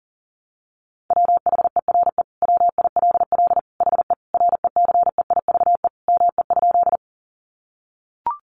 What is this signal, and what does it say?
Morse code, heard as a keyed tone, about 40 wpm, 700 Hz. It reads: WHERE WILL HE RECEIVE ME?
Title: Sentences from Top 1000 Words 40wpm